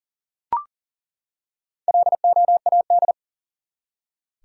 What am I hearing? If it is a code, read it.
Morse code, heard as a keyed tone, about 40 wpm, 700 Hz. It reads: LOAD
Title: Four Letter Words - 40wpm